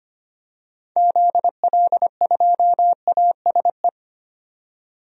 Morse code, heard as a keyed tone, about 25 wpm, 700 Hz. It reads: ZL2ASE
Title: DXCC-100 Call Signs - 6 Characters - 25wpm